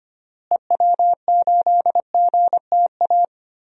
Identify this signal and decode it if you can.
Morse code, heard as a keyed tone, about 25 wpm, 700 Hz. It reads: EW8GTA
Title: DXCC-100 Call Signs - 6 Characters - Repeated 3x - 25wpm